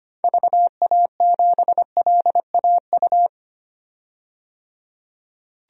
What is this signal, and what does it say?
Morse code, heard as a keyed tone, about 25 wpm, 700 Hz. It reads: VA7LAU
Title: DXCC-100 Call Signs - 6 Characters - 25wpm